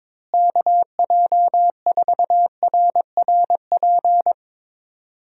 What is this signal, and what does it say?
Morse code, heard as a keyed tone, about 22 wpm, 700 Hz. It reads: KJ4RRP
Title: USA Call Signs - 6 Characters - Repeated 2x - 22wpm